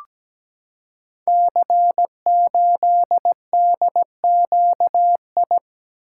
Morse code, heard as a keyed tone, about 17 wpm, 700 Hz. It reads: C8DQI
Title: Code Groups - 5 Letter:Numbers - 17wpm